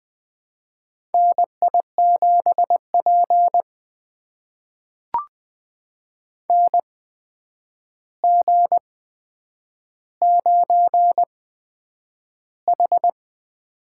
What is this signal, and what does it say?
Morse code, heard as a keyed tone, about 20 wpm, spaced as for 9 wpm, 700 Hz. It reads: NI7P NG9H
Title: USA Call Signs - 4 Characters - 8x Character Spacing - 20wpm